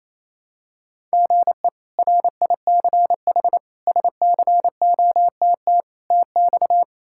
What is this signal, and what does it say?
Morse code, heard as a keyed tone, about 28 wpm, 700 Hz. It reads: GE RICH SCOTT TX
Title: K1USN SST QSO Practice - 28wpm